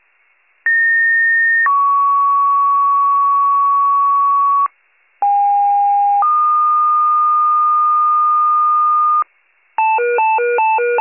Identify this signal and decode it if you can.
Two-tone page, heard as a one-second tone, then a three-second tone, about 1850 then 1120 Hz, about 810 then 1220 Hz. It is heard from 0.7 to 9.2 s.